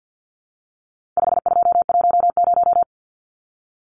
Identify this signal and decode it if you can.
Morse code, heard as a keyed tone, about 50 wpm, 700 Hz. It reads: 5210
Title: Numbers - 4-Digit 50wpm